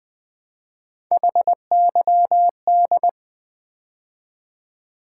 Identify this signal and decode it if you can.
Morse code, heard as a keyed tone, about 21 wpm, 700 Hz. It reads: HYD
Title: Build A Word - 3 Letters At A Time - 8-Letter Words - 20wpm